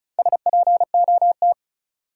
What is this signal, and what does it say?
Morse code, heard as a keyed tone, about 36 wpm, 700 Hz. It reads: SPOT
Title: Sentences from Top 1100 Words - New Word Drill Alt - 35wpm